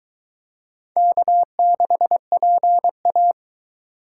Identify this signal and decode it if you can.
Morse code, heard as a keyed tone, about 23 wpm, 700 Hz. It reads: K6PA